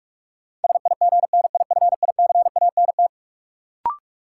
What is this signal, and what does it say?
Morse code, heard as a keyed tone, about 43 wpm, 700 Hz. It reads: SIGNIFICANT
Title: Sentences from Top 1600 Words - New Word Drill Alt - 45wpm